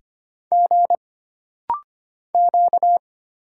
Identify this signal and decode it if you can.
Morse code, heard as a keyed tone, about 25 wpm, 700 Hz. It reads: G Q
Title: Single Letters - Rapid-Fire 25wpm